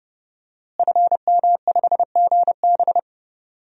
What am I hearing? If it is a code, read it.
Morse code, heard as a keyed tone, about 30 wpm, 700 Hz. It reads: FM5GB
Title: DXCC-100 Call Signs - 5 Characters - Repeated 3x - 30wpm